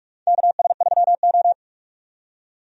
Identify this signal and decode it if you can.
Morse code, heard as a keyed tone, about 46 wpm, 700 Hz. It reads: KS3O
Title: USA Call Signs - 4 Characters - 45wpm